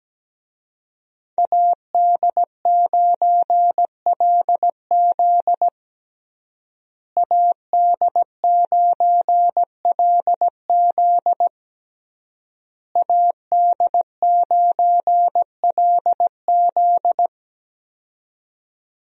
Morse code, heard as a keyed tone, about 17 wpm, 700 Hz. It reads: AD9LZ AD9LZ AD9LZ
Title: USA Call Signs - 5 Characters - Repeated 3x - 17wpm